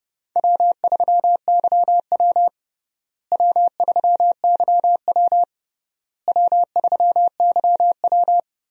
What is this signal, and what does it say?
Morse code, heard as a keyed tone, about 32 wpm, 700 Hz. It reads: W3YW W3YW W3YW
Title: USA Call Signs - 4 Characters - Repeated 3x - 30wpm